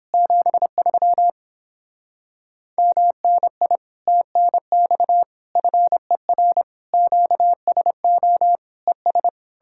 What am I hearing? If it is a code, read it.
Morse code, heard as a keyed tone, about 31 wpm, 700 Hz. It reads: 73 MNI TNX FER QSO ES